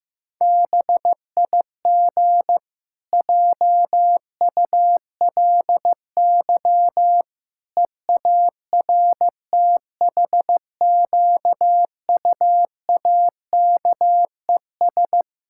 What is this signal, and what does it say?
Morse code, heard as a keyed tone, about 15 wpm, 700 Hz. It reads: BIG JULY EARTHQUAKES
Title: Pangram Challenge v2 - 15wpm